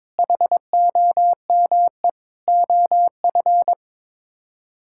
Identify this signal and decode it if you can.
Morse code, heard as a keyed tone, about 22 wpm, 700 Hz. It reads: HOME OF
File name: Sets of 9 Words - 22wpm